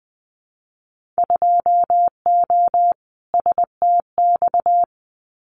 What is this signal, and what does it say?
Morse code, heard as a keyed tone, about 20 wpm, 700 Hz. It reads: 2O STX
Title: Winter Field Day Exchanges 20wpm